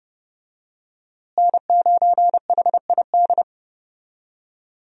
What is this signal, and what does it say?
Morse code, heard as a keyed tone, about 31 wpm, 700 Hz. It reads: N9HID
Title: USA Call Signs - 5 Characters - Repeated 2x - 30wpm